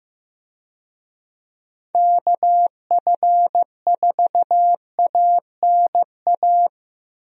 Morse code, heard as a keyed tone, about 15 wpm, 700 Hz. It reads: KF4ANA
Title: USA Call Signs - 6 Characters - Repeated 3x - 15wpm